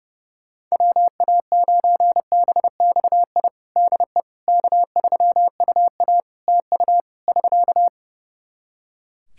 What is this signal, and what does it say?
Morse code, heard as a keyed tone, about 32 wpm, 700 Hz. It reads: WA9BXI DE K3UA TU <SK>
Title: K1USN SST QSO Practice 30wpm